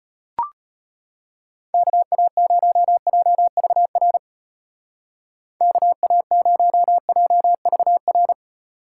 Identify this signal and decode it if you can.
Morse code, heard as a keyed tone, about 30 wpm, 700 Hz. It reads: KA0JVR KA0JVR
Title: Speed Racing - USA 2x3 Call Signs 25wpm